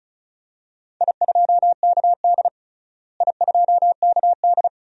Morse code, heard as a keyed tone, about 36 wpm, 700 Hz. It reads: I2KD I2KD
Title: DXCC-100 Call Signs - 4 Characters - Repeated 2x - 35wpm